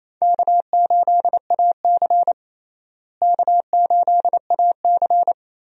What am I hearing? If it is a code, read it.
Morse code, heard as a keyed tone, about 28 wpm, 700 Hz. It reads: K8AC K8AC
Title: USA Call Signs - 4 Characters - Repeated 2x - 28wpm